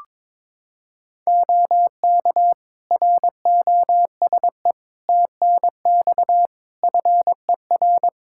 Morse code, heard as a keyed tone, about 22 wpm, 700 Hz. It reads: OK ROSE TNX FER
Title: Practice QSO - Exchange 4 - Ending - 22wpm